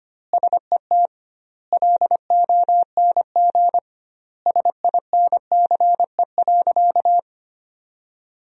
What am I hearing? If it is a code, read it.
Morse code, heard as a keyed tone, about 25 wpm, 700 Hz. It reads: SET LONG SINCE.
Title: 2x Word-Spacing - Sentences from Top 300 Words - 25wpm